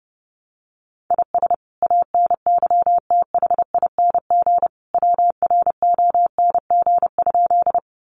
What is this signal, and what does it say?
Morse code, heard as a keyed tone, about 29 wpm, 700 Hz. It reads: IS ANYTHING WRONG?
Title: Sentences from Top 1000 Words 30wpm